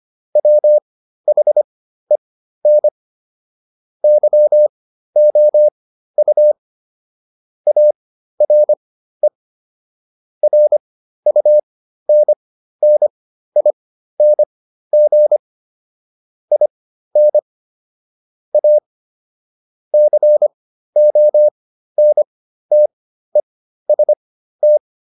Morse code, heard as a keyed tone, about 25 wpm, spaced as for 13 wpm, 600 Hz. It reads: WHEN YOU ARE RUNNING IN A CONTEST